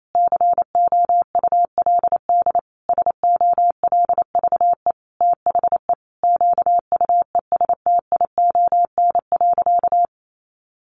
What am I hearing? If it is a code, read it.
Morse code, heard as a keyed tone, about 28 wpm, 700 Hz. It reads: COULD SOLVE THE QUESTION.
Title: Sentences from Top 1000 Words 28wpm